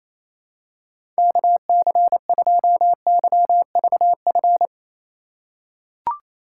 Morse code, heard as a keyed tone, about 27 wpm, 700 Hz. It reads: KC2YVF
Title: USA Call Signs - 6 Characters - 3x Character Spacing - 28wpm